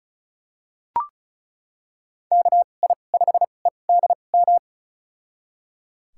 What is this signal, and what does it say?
Morse code, heard as a keyed tone, about 34 wpm, spaced as for 25 wpm, 700 Hz. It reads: KI5EDM
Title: USA Call Signs - 2x Character Spacing - 35wpm